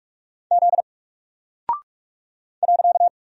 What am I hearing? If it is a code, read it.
Morse code, heard as a keyed tone, about 44 wpm, 700 Hz. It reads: Z .
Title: Single Character (Letter:Number:Punctuation:Prosign) 45wpm